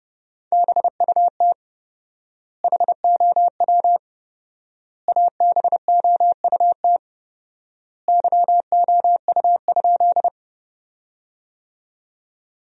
Morse code, heard as a keyed tone, about 32 wpm, 700 Hz. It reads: BUT HOW ABOUT YOU?
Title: 4x Word-Spacing - Sentences from Top 100 Words - 30wpm